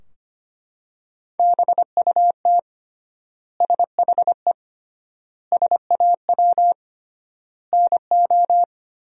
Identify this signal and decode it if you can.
Morse code, heard as a keyed tone, about 25 wpm, 700 Hz. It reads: BUT SHE SAW NO
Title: 3x Word-Spacing - Sentences from Top 300 Words - 25wpm